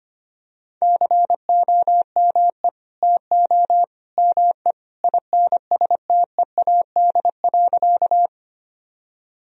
Morse code, heard as a keyed tone, about 25 wpm, 700 Hz. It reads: COME TO ME INSTEAD.
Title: Sentences from Top 1000 Words - Review Part 4 - 25wpm